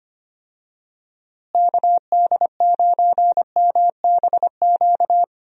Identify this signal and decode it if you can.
Morse code, heard as a keyed tone, about 25 wpm, 700 Hz. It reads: KD9MBQ
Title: USA Call Signs - 6 Characters - Repeated 2x - 25wpm